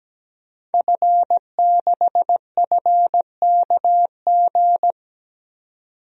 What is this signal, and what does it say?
Morse code, heard as a keyed tone, about 17 wpm, 700 Hz. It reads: F6FKG
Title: DXCC-100 Call Signs - 5 Characters - 17wpm